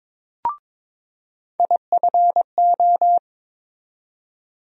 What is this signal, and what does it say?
Morse code, heard as a keyed tone, about 22 wpm, 700 Hz. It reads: IFO
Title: USA Call Sign Postfixes - 3 Letters 22wpm